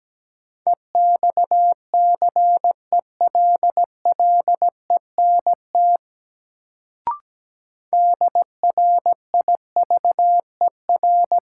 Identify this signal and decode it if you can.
Morse code, heard as a keyed tone, about 17 wpm, 700 Hz. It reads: EXCELLENT DRIVER
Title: Sentences from Top 1000 Words - New Word Drill - 17wpm